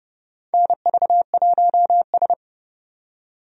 Morse code, heard as a keyed tone, about 30 wpm, 700 Hz. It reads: NV1S
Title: USA Call Signs - 4 Characters - 30wpm